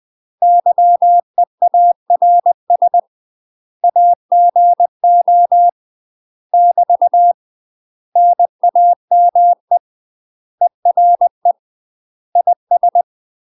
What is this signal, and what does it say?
Morse code, heard as a keyed tone, about 20 wpm, 700 Hz. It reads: YEARS AGO <BT> NAME ERE IS
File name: UK MASTER QSOs TRAINING / 20WPM / UK MASTER QSOs 0035